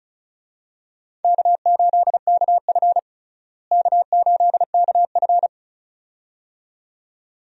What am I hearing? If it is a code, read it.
Morse code, heard as a keyed tone, about 34 wpm, 700 Hz. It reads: K8KF K8KF
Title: USA Call Signs - 4 Characters - Repeated 2x - 35wpm